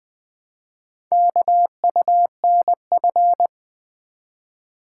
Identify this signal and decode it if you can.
Morse code, heard as a keyed tone, about 20 wpm, 700 Hz. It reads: KUNF
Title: Four Letter-Number Combinations - 20wpm